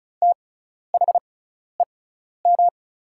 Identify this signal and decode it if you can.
Morse code, heard as a keyed tone, about 36 wpm, spaced as for 12 wpm, 700 Hz. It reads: THEM
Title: 6x Character Spacing - Top 100 Words - 35wpm